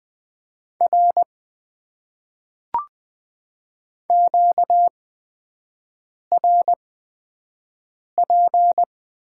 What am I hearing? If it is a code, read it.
Morse code, heard as a keyed tone, about 20 wpm, spaced as for 5 wpm, 700 Hz. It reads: R QRP
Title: Top 90 QSO Elements - 8x Character Spacing - 20wpm